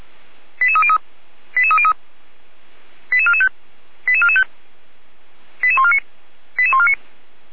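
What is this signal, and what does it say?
Five-tone selective call (ZVEI), heard as quick runs of five tones, tones of about 70 ms. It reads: 88303 88303 88445 88445 88149 88149
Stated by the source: ZVEI five-tone call received on SDR.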